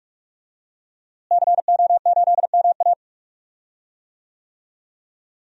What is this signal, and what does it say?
Morse code, heard as a keyed tone, about 43 wpm, 700 Hz. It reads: CO8MA
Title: DXCC-100 Call Signs - 5 Characters - Repeated 2x - 45wpm